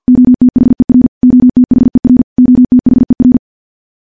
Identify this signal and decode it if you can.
Morse code, heard as a keyed tone, about 50 wpm, 250 Hz. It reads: OTHER OTHER OTHER
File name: morse / morse-training-50WPM